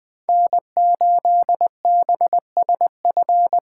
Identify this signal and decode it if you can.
Morse code, heard as a keyed tone, about 21 wpm, 700 Hz. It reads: N8BSF
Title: USA Callsign DE Callsign - 5 Characters - 20wpm